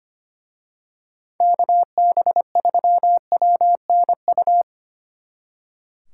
Morse code, heard as a keyed tone, about 29 wpm, 700 Hz. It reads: KB3WNU
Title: Speed Racing - USA 2x3 Call Signs 25wpm